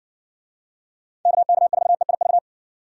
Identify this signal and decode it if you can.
Morse code, heard as a keyed tone, about 60 wpm, 700 Hz. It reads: KK4EIV